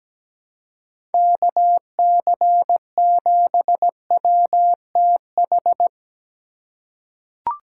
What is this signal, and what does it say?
Morse code, heard as a keyed tone, about 17 wpm, 700 Hz. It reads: KC7WTH
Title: USA Call Signs - 6 Characters - 3x Character Spacing - 17wpm